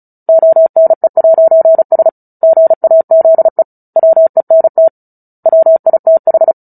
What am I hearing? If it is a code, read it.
Morse code, heard as a keyed tone, about 34 wpm, 650 Hz. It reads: ONE'S GAZE WENT WITH
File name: Book Two / 09-Chapter 9 - 35 WPM